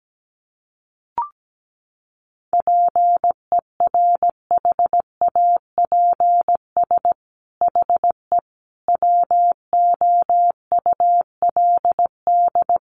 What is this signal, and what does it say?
Morse code, heard as a keyed tone, about 17 wpm, 700 Hz. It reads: PERHAPS HE WOULD
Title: Sentences from Top 1300 Words 17wpm